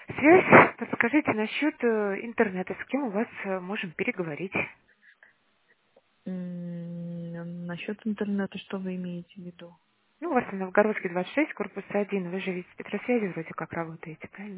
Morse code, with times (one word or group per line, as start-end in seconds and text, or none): none